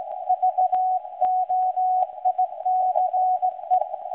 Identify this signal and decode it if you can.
Morse code, heard as a keyed tone, about 17 wpm, 740 Hz. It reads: VOICE
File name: WebSDR records / SAQ recording start 17.2kHz